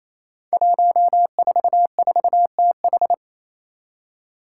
Morse code, heard as a keyed tone, about 27 wpm, 700 Hz. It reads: 144TH
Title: Five Challenging Letter-Number Combinations 28wpm